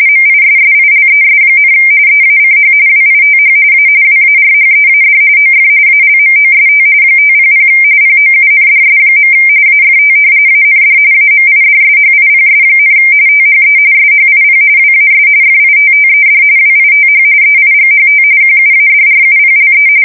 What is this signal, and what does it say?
RTTY, about 45.45 baud. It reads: MISSILES, EACH WITH A WARHEAD OF ALMOST 500 POUNDS. AFT AND AMIDSHIPS, THE SHIP WAS FITTED WITH EIGHT QUADRUPLE
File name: fsk170 / battleshipnj